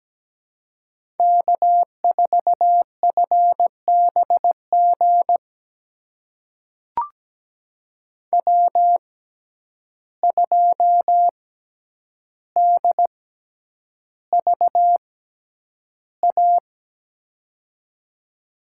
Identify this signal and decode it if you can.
Morse code, heard as a keyed tone, about 17 wpm, spaced as for 9 wpm, 700 Hz. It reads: K4FBG W2DVA
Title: USA Call Signs - 5 Characters - 6x Character Spacing - 17wpm